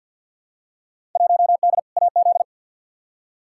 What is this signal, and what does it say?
Morse code, heard as a keyed tone, about 47 wpm, 700 Hz. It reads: 1D AZ
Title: Field Day Exchanges - 50wpm